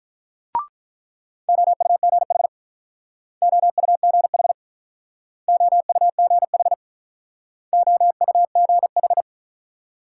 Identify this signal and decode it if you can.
Morse code, heard as a keyed tone, about 45 wpm, 700 Hz. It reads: OUGH OUGH OUGH OUGH